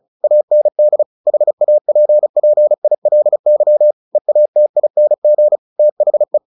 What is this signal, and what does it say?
Morse code, heard as a keyed tone, about 34 wpm, 600 Hz. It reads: AND HAPPILY EATING THE
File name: the old man and the sea / the old man and the sea 1 35wpm